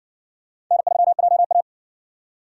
Morse code, heard as a keyed tone, about 60 wpm, 700 Hz. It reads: N3JA